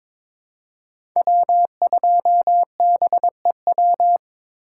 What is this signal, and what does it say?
Morse code, heard as a keyed tone, about 22 wpm, 700 Hz. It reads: W2BEW
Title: USA Call Signs - 5 Characters - 2x Character Spacing - 22wpm